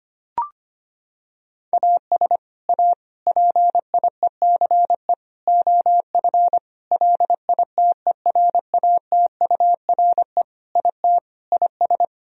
Morse code, heard as a keyed tone, about 25 wpm, 700 Hz. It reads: AS A PIECE OF LITERATURE IT IS
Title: Sentences from Top 1000 Words - Review Part 3 - 25wpm